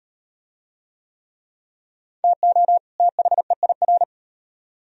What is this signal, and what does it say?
Morse code, heard as a keyed tone, about 38 wpm, 700 Hz. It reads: TO THEIR